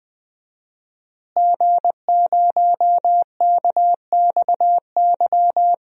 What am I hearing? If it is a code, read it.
Morse code, heard as a keyed tone, about 20 wpm, 700 Hz. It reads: G0KXY
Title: DXCC-100 Call Signs - 5 Characters - Repeated 2x - 20wpm